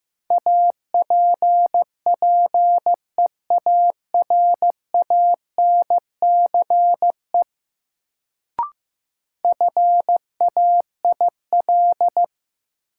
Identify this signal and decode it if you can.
Morse code, heard as a keyed tone, about 15 wpm, 700 Hz. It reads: APPEARANCE FAIL
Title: Sentences from Top 1100 Words - New Word Drill Alt - 15wpm